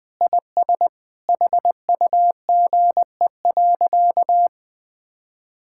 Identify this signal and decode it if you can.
Morse code, heard as a keyed tone, about 21 wpm, 700 Hz. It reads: IS HUGE.
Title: Sentences with 5-Word Phrases - Spoken Beginning - 20wpm